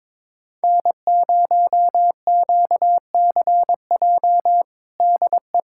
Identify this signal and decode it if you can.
Morse code, heard as a keyed tone, about 22 wpm, 700 Hz. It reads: N0QCJ DE
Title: Practice QSO - Exchange 4 - Ending - 22wpm